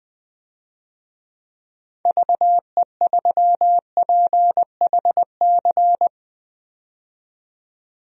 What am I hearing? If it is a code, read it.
Morse code, heard as a keyed tone, about 20 wpm, 700 Hz. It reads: VE3PHC
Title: DXCC-100 Call Signs - 6 Characters - 20wpm